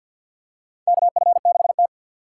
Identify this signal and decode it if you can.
Morse code, heard as a keyed tone, about 53 wpm, 700 Hz. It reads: KF6T
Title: K1USN SST QSO Practice - 50wpm